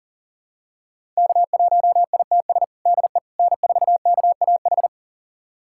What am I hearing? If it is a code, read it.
Morse code, heard as a keyed tone, about 37 wpm, 700 Hz. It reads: K1ITS DE N4KAH
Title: USA Callsign DE Callsign - 5 Characters - 40wpm